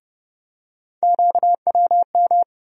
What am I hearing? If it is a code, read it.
Morse code, heard as a keyed tone, about 30 wpm, 700 Hz. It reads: QWM
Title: USA Call Sign Postfixes - 3 Letters 30wpm